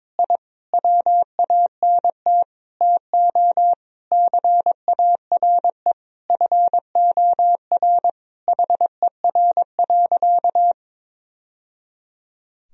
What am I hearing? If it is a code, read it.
Morse code, heard as a keyed tone, about 22 wpm, 700 Hz. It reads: I WANT TO CARE FOR HER.
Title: Sentences from Top 1000 Words - Review Part 2 - 22wpm